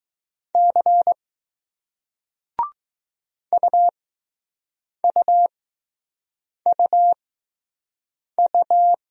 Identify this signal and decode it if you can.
Morse code, heard as a keyed tone, about 20 wpm, spaced as for 6 wpm, 700 Hz. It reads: C UUUU